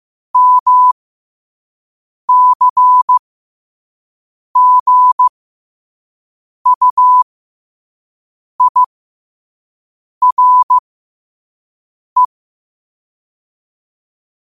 Morse code, heard as a keyed tone, about 15 wpm, spaced as for 5 wpm, 1000 Hz. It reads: MCGUIRE